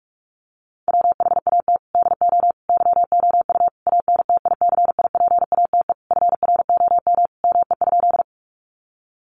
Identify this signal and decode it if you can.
Morse code, heard as a keyed tone, about 44 wpm, 700 Hz. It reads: WHAT DO YOU ANTICIPATE FROM ME?
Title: Sentences from Top 1500 Words 45wpm